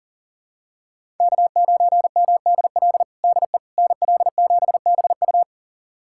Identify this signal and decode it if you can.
Morse code, heard as a keyed tone, about 37 wpm, 700 Hz. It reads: K9MDL DE NL7BU
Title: USA Callsign DE Callsign - 5 Characters - 40wpm